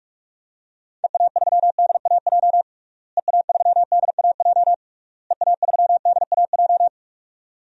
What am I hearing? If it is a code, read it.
Morse code, heard as a keyed tone, about 46 wpm, 700 Hz. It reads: EA3DAJ EA3DAJ EA3DAJ
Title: DXCC-100 Call Signs - 6 Characters - Repeated 3x - 45wpm